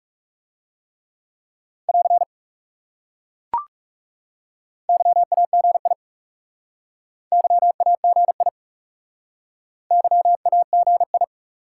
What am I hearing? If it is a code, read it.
Morse code, heard as a keyed tone, about 40 wpm, 700 Hz. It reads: <AR> YAGI YAGI YAGI